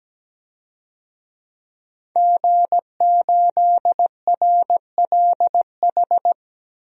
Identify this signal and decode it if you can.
Morse code, heard as a keyed tone, about 17 wpm, 700 Hz. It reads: G8RLH
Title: DXCC-100 Call Signs - 5 Characters - 17wpm